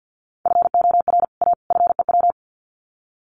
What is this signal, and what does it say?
Morse code, heard as a keyed tone, about 50 wpm, 700 Hz. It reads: FOR A FEW
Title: Sets of 3 Words 50wpm